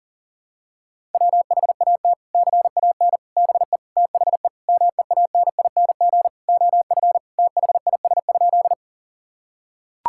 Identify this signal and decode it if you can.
Morse code, heard as a keyed tone, about 40 wpm, 700 Hz. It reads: WHAT CAN BE THE MEANING OF THIS?
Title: Sentences from Top 1000 Words - Review Part 6 - 40wpm